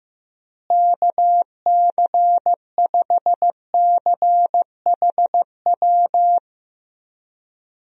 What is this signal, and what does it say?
Morse code, heard as a keyed tone, about 15 wpm, 700 Hz. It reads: KC5CHW
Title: USA Call Signs - 6 Characters - Repeated 3x - 15wpm